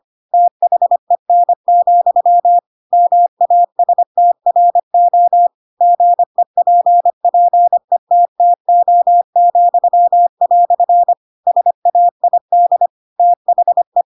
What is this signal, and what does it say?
Morse code, heard as a keyed tone, about 25 wpm, 700 Hz. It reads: THEN, MASTRO GEPPETTO," SAID THE
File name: Pinocchio / Pinocchio ch2 25 25 ik1hge